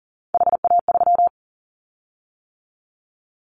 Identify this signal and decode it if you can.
Morse code, heard as a keyed tone, about 42 wpm, 700 Hz. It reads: HA3
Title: Callsign Prefixes for Top 25 DXCC - 40wpm